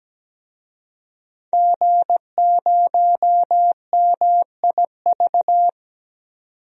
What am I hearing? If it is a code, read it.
Morse code, heard as a keyed tone, about 17 wpm, 700 Hz. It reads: G0MIV
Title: DXCC-100 Call Signs - 5 Characters - 17wpm